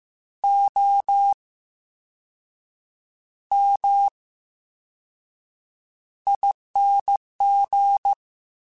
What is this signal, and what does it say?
Morse code, heard as a keyed tone, about 15 wpm, spaced as for 6 wpm, 800 Hz. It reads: OMING
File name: Special INGs-mp3s / Special INGs-mp3s / VerySlowToFaster Ing @13